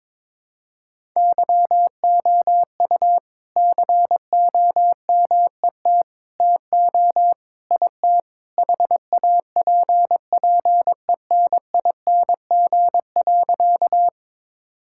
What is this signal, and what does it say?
Morse code, heard as a keyed tone, about 22 wpm, 700 Hz. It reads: YOU COMET TO IT HAPPENING.
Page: morse-code-ninja-practice-22wpm